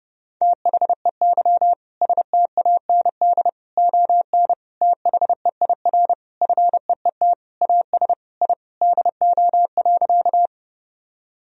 Morse code, heard as a keyed tone, about 32 wpm, 700 Hz. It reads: THEY STAND ON THEIR FEET AS I DO.
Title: Sentences from Top 1000 Words - Review Part 1 - 30wpm